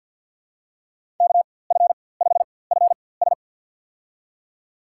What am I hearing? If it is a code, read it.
Morse code, heard as a keyed tone, about 50 wpm, spaced as for 23 wpm, 690 Hz. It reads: KF5FS
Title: USA Call Signs - 5 Characters - 4x Character Spacing - 50wpm